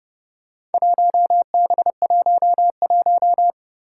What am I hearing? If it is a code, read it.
Morse code, heard as a keyed tone, about 30 wpm, 700 Hz. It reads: 1B11
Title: Four Challenging Letter-Number Combinations 30wpm